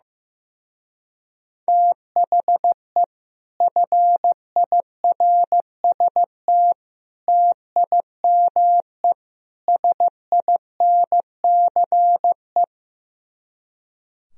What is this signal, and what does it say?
Morse code, heard as a keyed tone, about 15 wpm, 700 Hz. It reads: THE FIRST TIME SINCE
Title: Speed-Racing - Sets of 4 Words 15wpm